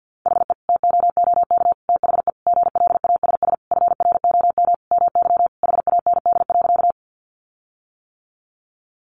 Morse code, heard as a keyed tone, about 50 wpm, 700 Hz. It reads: HE TOOK THE GLASS FROM MY HAND.
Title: Sentences from Top 1300 Words 50wpm